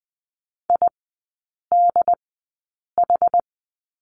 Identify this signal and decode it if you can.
Morse code, heard as a keyed tone, about 20 wpm, spaced as for 8 wpm, 700 Hz. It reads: IDH
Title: 100 Lesson 17e - Letter u - Three Characters 20wpm